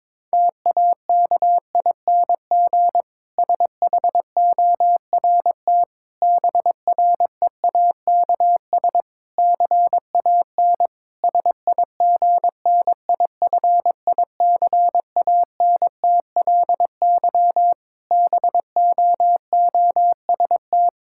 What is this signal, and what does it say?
Morse code, heard as a keyed tone, about 22 wpm, 700 Hz. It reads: TAKING SHORT BREAKS CAN SIGNIFICANTLY BOOST
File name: Difficult Sentences with Hints - 22wpm